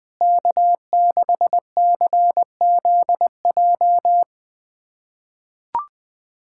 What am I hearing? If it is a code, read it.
Morse code, heard as a keyed tone, about 21 wpm, 700 Hz. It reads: K6CZJ
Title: USA Callsign DE Callsign - 5 Characters - 20wpm